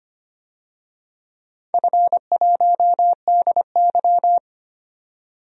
Morse code, heard as a keyed tone, about 25 wpm, 700 Hz. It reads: F1DY
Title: DXCC-100 Call Signs - 4 Characters - 25wpm